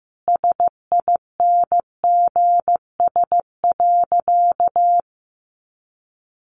Morse code, heard as a keyed tone, about 15 wpm, 700 Hz. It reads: SINGS.
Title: Easy Phrases 15wpm